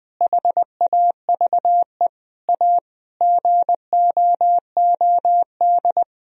Morse code, heard as a keyed tone, about 20 wpm, 700 Hz. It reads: HAVE A GOOD
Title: Sentences from Top 1000 Words - Review Part 2 - 20wpm